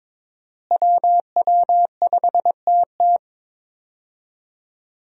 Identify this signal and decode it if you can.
Morse code, heard as a keyed tone, about 22 wpm, 700 Hz. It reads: WW5TT
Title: USA Call Signs - 5 Characters - Repeated 3x - 22wpm